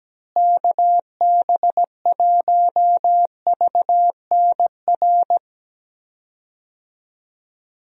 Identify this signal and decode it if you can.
Morse code, heard as a keyed tone, about 17 wpm, 700 Hz. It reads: KB1VNR
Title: USA Call Signs - 6 Characters - Repeated 3x - 17wpm